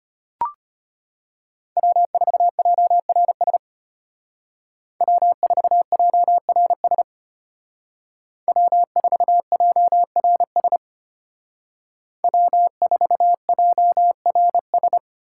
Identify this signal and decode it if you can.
Morse code, heard as a keyed tone, about 33 wpm, 700 Hz. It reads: W4JRS W4JRS W4JRS W4JRS